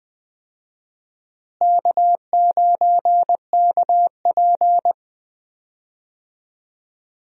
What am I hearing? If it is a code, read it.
Morse code, heard as a keyed tone, about 20 wpm, 700 Hz. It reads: K9KP
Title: USA Call Signs - 4 Characters - Repeated 2x - 20wpm